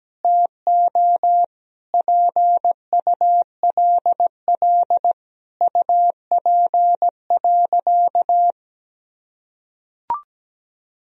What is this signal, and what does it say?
Morse code, heard as a keyed tone, about 17 wpm, 700 Hz. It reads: TO PULL UP.
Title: Sentences from Top 1000 Words - Review Part 2 - 17wpm